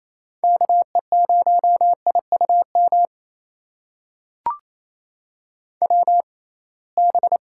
Morse code, heard as a keyed tone, about 28 wpm, spaced as for 21 wpm, 700 Hz. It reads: KE0IUM WB
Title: USA Call Signs - 6 Characters - 6x Character Spacing - 28wpm